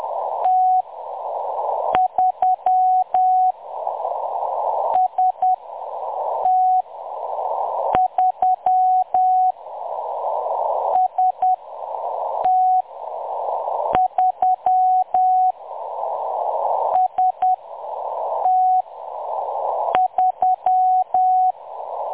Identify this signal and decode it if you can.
Morse code, heard as a keyed tone, about 10 wpm, spaced as for 6 wpm, 740 Hz. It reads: T3 ST3 ST3 ST3
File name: M23-ST3- -0858utc-10184khz